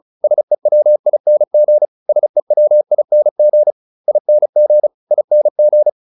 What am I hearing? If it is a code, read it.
Morse code, heard as a keyed tone, about 34 wpm, 600 Hz. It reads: SEWING SEWING ING ING